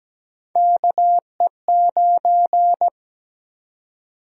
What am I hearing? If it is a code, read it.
Morse code, heard as a keyed tone, about 17 wpm, 700 Hz. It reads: KE9
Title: USA Call Sign Prefixes - 17wpm